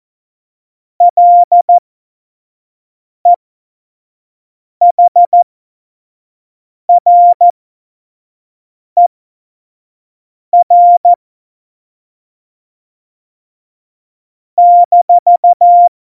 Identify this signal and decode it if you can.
Morse code, heard as a keyed tone, about 14 wpm, spaced as for 5 wpm, 700 Hz. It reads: LEHRER -
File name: Morsekurs / Witze / Witz 23. Juli 2019, 25 F70